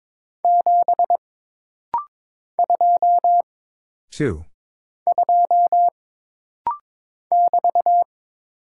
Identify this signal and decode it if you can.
Morse code, heard as a keyed tone, about 22 wpm, 700 Hz. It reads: 7 2 2 <BT>
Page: This rapid-fire timing is geared to push your ICR (Instant Character Recognition) skills.